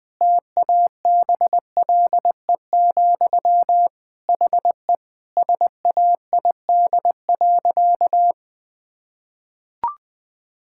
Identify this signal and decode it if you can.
Morse code, heard as a keyed tone, about 20 wpm, 700 Hz. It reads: TABLE, HE SAID.
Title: Sentences from Top 1000 Words - Review Part 6 - 20wpm